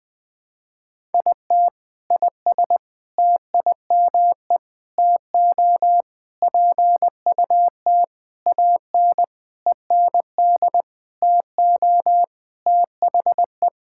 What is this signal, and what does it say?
Morse code, heard as a keyed tone, about 20 wpm, 700 Hz. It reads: IT IS TIME TO PUT AN END TO THE
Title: Sentences with 6-Word Phrases - 20wpm